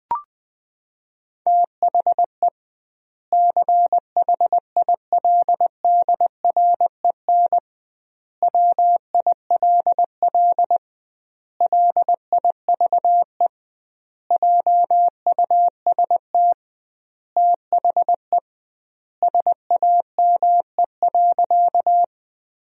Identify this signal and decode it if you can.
Morse code, heard as a keyed tone, about 20 wpm, 700 Hz. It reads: THE CHILDREN WILL LIVE JUST THE SAME.
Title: 2x Word-Spacing - Sentences from Top 300 Words - 20wpm